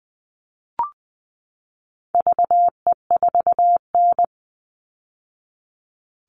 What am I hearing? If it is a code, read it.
Morse code, heard as a keyed tone, about 20 wpm, 700 Hz. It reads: VE4N
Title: All Cut Numbers - 4-Digit 20wpm